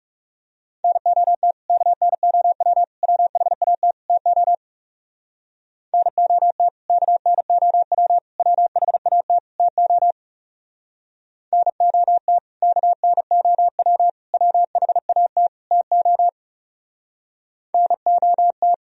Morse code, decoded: NOT KNOW WHAT TO NOT KNOW WHAT TO NOT KNOW WHAT TO NOT

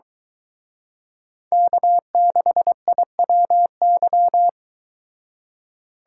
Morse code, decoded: K6IWY

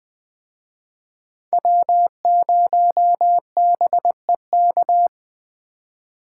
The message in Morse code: W0BEK

